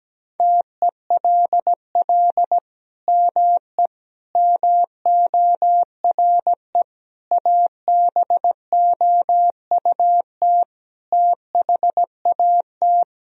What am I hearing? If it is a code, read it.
Morse code, heard as a keyed tone, about 17 wpm, 700 Hz. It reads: TELL ME MORE ABOUT THAT
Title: Sentences from Top 1000 Words - Review Part 4 - 17wpm